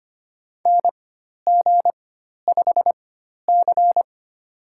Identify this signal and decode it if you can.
Morse code, heard as a keyed tone, about 25 wpm, spaced as for 12 wpm, 700 Hz. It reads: NG5C
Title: USA Call Signs - 4x Character Spacing - 25wpm